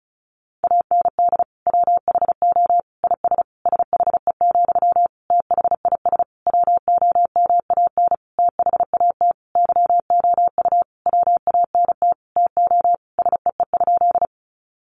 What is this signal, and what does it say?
Morse code, heard as a keyed tone, about 34 wpm, 700 Hz. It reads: AND WHO IS SHE, THIS WOMAN THAT YOU WANT TO SEE?
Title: Sentences from Top 1100 Words 35wpm